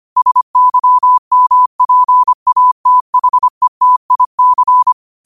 Morse code, heard as a keyed tone, about 25 wpm, 1000 Hz. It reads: SYMPATHETIC
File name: qotdfiles / 25wpm